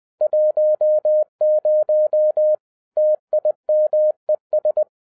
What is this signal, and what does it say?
Morse code, heard as a keyed tone, about 20 wpm, 610 Hz. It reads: 10 TIMES
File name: 2011-10-mp3 / n03-TARO JR0QWW 20wpm